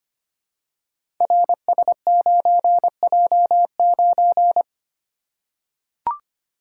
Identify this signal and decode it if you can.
Morse code, heard as a keyed tone, about 25 wpm, 700 Hz. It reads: RS9J9
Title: Code Groups - 5 Letter:Numbers - 25wpm